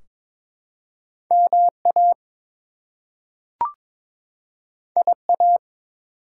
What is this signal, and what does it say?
Morse code, heard as a keyed tone, about 22 wpm, 700 Hz. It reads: MA IA
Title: US States and Canadian Provinces 22wpm